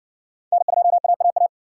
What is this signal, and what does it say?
Morse code, heard as a keyed tone, about 60 wpm, 700 Hz. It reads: N2AAA